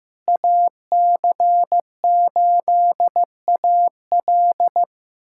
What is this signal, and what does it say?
Morse code, heard as a keyed tone, about 16 wpm, 700 Hz. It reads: AC8AL
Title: USA Callsign DE Callsign - 5 Characters - 15wpm